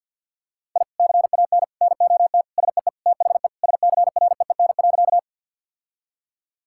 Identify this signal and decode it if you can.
Morse code, heard as a keyed tone, about 50 wpm, 700 Hz. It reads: I CAN NOT SEE THE SCREEN.
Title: Sentences from Top 1000 Words - Review Part 6 - 50wpm